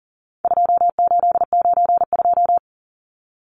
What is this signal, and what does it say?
Morse code, heard as a keyed tone, about 40 wpm, 700 Hz. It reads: 2892
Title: Numbers - 4-Digit 40wpm